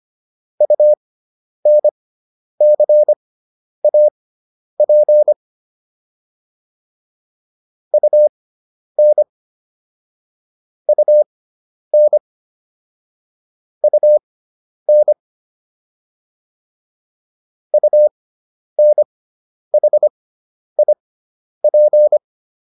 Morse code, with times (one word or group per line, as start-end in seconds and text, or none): 0.6-5.3 UNCAP
7.9-9.2 UN
10.9-12.2 UN
13.8-15.1 UN
17.7-22.2 UNHIP